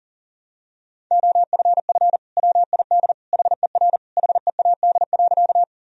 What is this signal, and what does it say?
Morse code, heard as a keyed tone, about 40 wpm, 700 Hz. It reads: OFF WID HER HEAD.
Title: Sentences from Top 300 Words and Abbreviations 40wpm